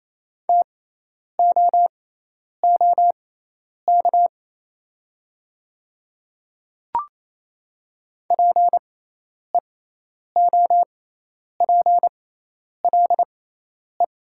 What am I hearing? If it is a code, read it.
Morse code, heard as a keyed tone, about 27 wpm, spaced as for 10 wpm, 690 Hz. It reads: TOOK PEOPLE
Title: 6x Character Spacing - Top 300 Words - 28wpm